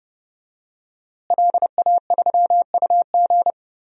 Morse code, heard as a keyed tone, about 30 wpm, 700 Hz. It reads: LA3UG
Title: DXCC-100 Call Signs - 5 Characters - Repeated 2x - 30wpm